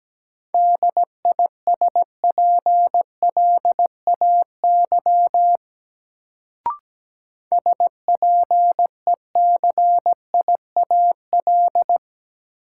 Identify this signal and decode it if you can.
Morse code, heard as a keyed tone, about 17 wpm, 700 Hz. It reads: DISPLAY SPECIAL
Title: Sentences from Top 1200 Words - New Word Drill Alt - 17wpm